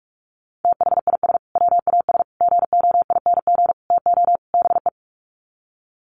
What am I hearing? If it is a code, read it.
Morse code, heard as a keyed tone, about 44 wpm, 700 Hz. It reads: THIS WAS GOING TO BE
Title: Sets of 5 Words Encore2 45wpm